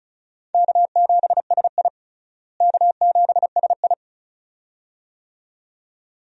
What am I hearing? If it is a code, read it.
Morse code, heard as a keyed tone, about 34 wpm, 700 Hz. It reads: K7SI K7SI
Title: USA Call Signs - 4 Characters - Repeated 2x - 35wpm